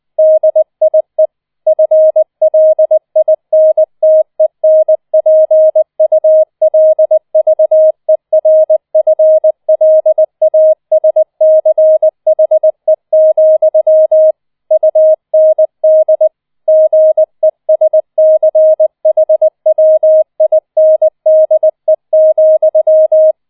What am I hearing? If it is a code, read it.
Morse code, heard as a keyed tone, about 20 wpm, 630 Hz. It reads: DIE FLINTENPULVERFLASCHE, UND GESCHWINDE,